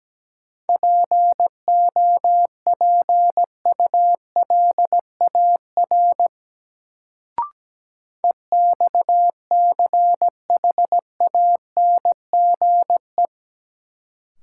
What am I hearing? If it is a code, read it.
Morse code, heard as a keyed tone, about 17 wpm, 700 Hz. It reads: POPULAR EXCHANGE
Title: Sets of 5 Words - New Word Drill - 17wpm